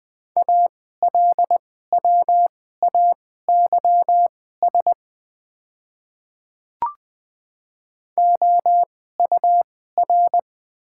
Morse code, heard as a keyed tone, about 20 wpm, spaced as for 14 wpm, 700 Hz. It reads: ALWAYS OUR
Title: 2x Character Spacing - Top 300 Words - 20wpm